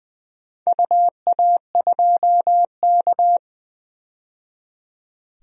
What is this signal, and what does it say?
Morse code, heard as a keyed tone, about 20 wpm, 700 Hz. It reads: UA2K
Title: DXCC-100 Call Signs - 4 Characters - 20wpm